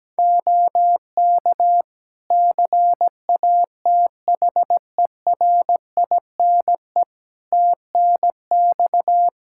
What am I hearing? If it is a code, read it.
Morse code, heard as a keyed tone, about 17 wpm, 700 Hz. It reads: OK CATHERINE TNX
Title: Practice QSO - Exchange 4 - Ending - 17wpm